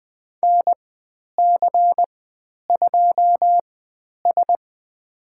Morse code, heard as a keyed tone, about 20 wpm, spaced as for 10 wpm, 700 Hz. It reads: NC2S
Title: 102 Lesson 17g - Letter u - Call Signs 20wpm